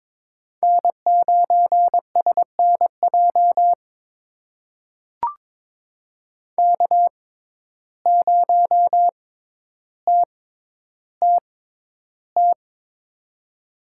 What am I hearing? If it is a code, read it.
Morse code, heard as a keyed tone, about 22 wpm, spaced as for 11 wpm, 700 Hz. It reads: N9SNJ K0TTT
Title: USA Call Signs - 5 Characters - 6x Character Spacing - 22wpm